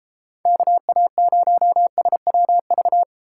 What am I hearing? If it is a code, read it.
Morse code, heard as a keyed tone, about 32 wpm, 700 Hz. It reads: KA0SWV